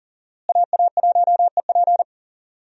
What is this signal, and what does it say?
Morse code, heard as a keyed tone, about 40 wpm, 700 Hz. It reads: AA1EP
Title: USA Call Signs - 5 Characters - Repeated 2x - 40wpm